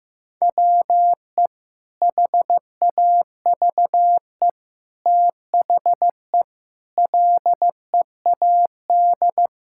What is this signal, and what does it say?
Morse code, heard as a keyed tone, about 15 wpm, 700 Hz. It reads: WE HAVE THE LEAD
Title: Sentences from Top 1000 Words - Review Part 5 - 15wpm